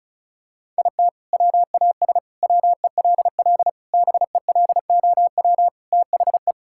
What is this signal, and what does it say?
Morse code, heard as a keyed tone, about 36 wpm, 700 Hz. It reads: IT WAS WELL BELOW THE
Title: Sentences from Top 1000 Words - Review Part 6 - 35wpm